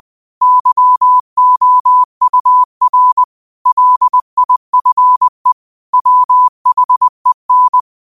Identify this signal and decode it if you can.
Morse code, heard as a keyed tone, about 20 wpm, 1000 Hz. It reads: YOUR LIFE WHEN
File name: qotdfiles / 20wpm